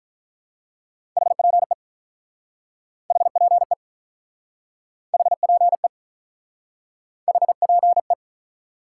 Morse code, heard as a keyed tone, about 48 wpm, 700 Hz. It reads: HPE HPE HPE HPE